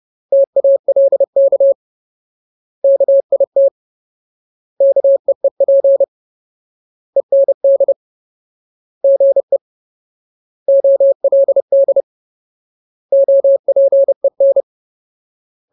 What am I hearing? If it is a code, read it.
Morse code, heard as a keyed tone, about 30 wpm, 550 Hz. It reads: TALK KIT KEEP END GE OLD OPEN
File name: Ultimate Word List / 30WPM Ultimate